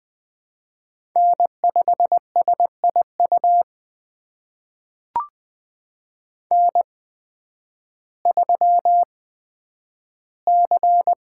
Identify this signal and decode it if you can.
Morse code, heard as a keyed tone, about 20 wpm, spaced as for 11 wpm, 700 Hz. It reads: N5SIU N3C
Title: USA Call Signs - 5 Characters - 8x Character Spacing - 20wpm